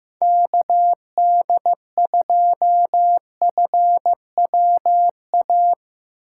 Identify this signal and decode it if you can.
Morse code, heard as a keyed tone, about 15 wpm, 700 Hz. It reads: KD2FWA
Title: USA Call Signs - 6 Characters - 4x Character Spacing - 15wpm